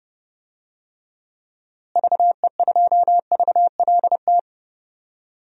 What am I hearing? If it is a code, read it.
Morse code, heard as a keyed tone, about 31 wpm, 700 Hz. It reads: VE2VLT
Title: DXCC-100 Call Signs - 6 Characters - Repeated 2x - 30wpm